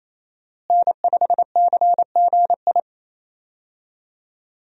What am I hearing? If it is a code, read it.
Morse code, heard as a keyed tone, about 27 wpm, 700 Hz. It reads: N5CGI
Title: USA Call Signs - 5 Characters - Repeated 2x - 28wpm